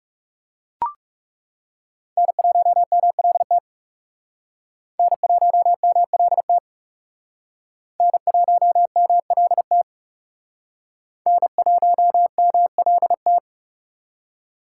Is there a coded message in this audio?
Yes: Morse code from 2.2 to 13.4 s, heard as a keyed tone, about 40 wpm, 700 Hz. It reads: N1MLT N1MLT N1MLT N1MLT